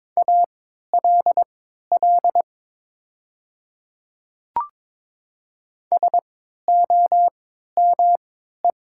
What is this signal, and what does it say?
Morse code, heard as a keyed tone, about 23 wpm, spaced as for 13 wpm, 700 Hz. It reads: ALL SOME
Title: 3x Character Spacing - Top 500 Words - 22wpm